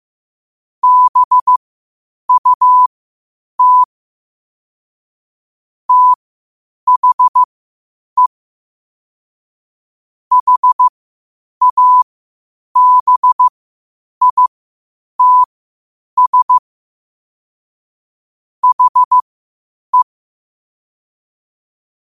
Morse code, heard as a keyed tone, about 15 wpm, spaced as for 8 wpm, 1000 Hz. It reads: BUT THE HABITS HE
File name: qotdfiles / 8wpm